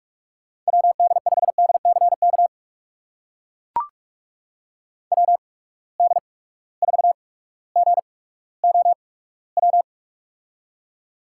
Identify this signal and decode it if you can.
Morse code, heard as a keyed tone, about 42 wpm, spaced as for 19 wpm, 700 Hz. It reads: WD5DCK WD4GOW
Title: USA Call Signs - 6 Characters - 8x Character Spacing - 45wpm